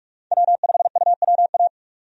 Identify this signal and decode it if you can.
Morse code, heard as a keyed tone, about 44 wpm, 700 Hz. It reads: W5UWA
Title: USA Call Signs - 5 Characters - 45wpm